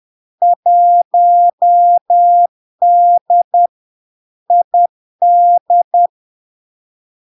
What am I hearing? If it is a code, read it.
Morse code, heard as a keyed tone, about 10 wpm, 700 Hz. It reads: 1D ID
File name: LARK-FD-2022-Log-in-CW-10wpm-700-Hz